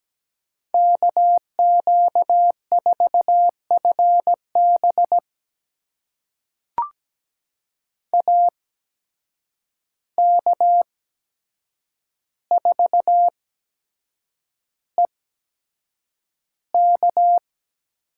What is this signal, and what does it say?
Morse code, heard as a keyed tone, about 17 wpm, spaced as for 7 wpm, 700 Hz. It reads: KQ4FB AK4EK
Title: USA Call Signs - 5 Characters - 8x Character Spacing - 17wpm